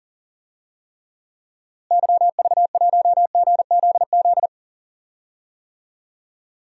Morse code, heard as a keyed tone, about 40 wpm, 700 Hz. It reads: YV1GZZ
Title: DXCC-100 Call Signs - 6 Characters - Repeated 2x - 40wpm